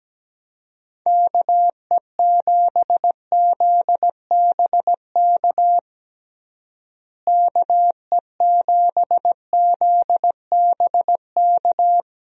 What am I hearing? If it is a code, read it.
Morse code, heard as a keyed tone, about 17 wpm, 700 Hz. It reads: KE7ZBK KE7ZBK
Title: USA Call Signs - 6 Characters - Repeated 3x - 17wpm